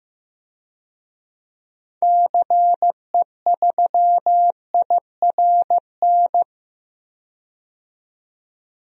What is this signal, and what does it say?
Morse code, heard as a keyed tone, about 15 wpm, 700 Hz. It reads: CE3IRN
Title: DXCC-100 Call Signs - 6 Characters - 15wpm